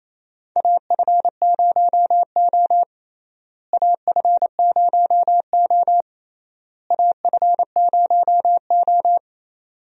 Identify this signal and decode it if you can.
Morse code, heard as a keyed tone, about 27 wpm, 700 Hz. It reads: AF0O AF0O AF0O
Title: USA Call Signs - 4 Characters - Repeated 3x - 28wpm